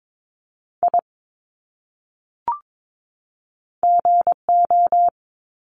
Morse code, heard as a keyed tone, about 22 wpm, 700 Hz. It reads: I GO
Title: Top 300 Words 22wpm